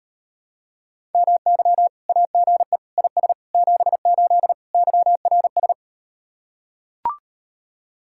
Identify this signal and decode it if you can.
Morse code, heard as a keyed tone, about 39 wpm, 700 Hz. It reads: MY AGE IS 78 YRS